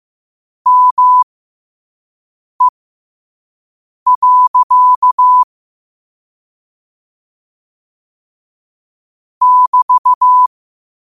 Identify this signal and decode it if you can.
Morse code, heard as a keyed tone, about 15 wpm, spaced as for 5 wpm, 1000 Hz. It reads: ME. <BT>
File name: qotdfiles / 5wpm